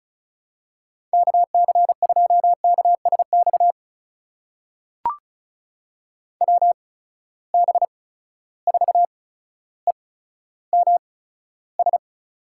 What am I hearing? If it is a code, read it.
Morse code, heard as a keyed tone, about 34 wpm, spaced as for 15 wpm, 700 Hz. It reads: KC2KSX WB4EMS
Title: USA Call Signs - 6 Characters - 8x Character Spacing - 35wpm